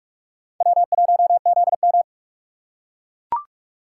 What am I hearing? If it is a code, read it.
Morse code, heard as a keyed tone, about 42 wpm, 700 Hz. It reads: W1ZM